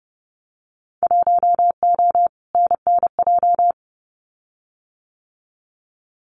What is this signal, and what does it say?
Morse code, heard as a keyed tone, about 30 wpm, 700 Hz. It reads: 1O NNJ
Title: Winter Field Day Exchanges 30wpm